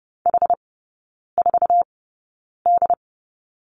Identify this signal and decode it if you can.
Morse code, heard as a keyed tone, about 30 wpm, spaced as for 9 wpm, 700 Hz. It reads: H4D
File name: Lesson 16e - Letter c - Three Characters 30wpm